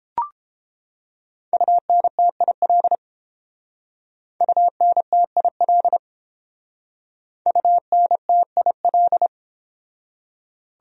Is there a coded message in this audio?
Yes: Morse code from 1.5 to 9.3 s, heard as a keyed tone, about 30 wpm, 700 Hz. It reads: UNTIL UNTIL UNTIL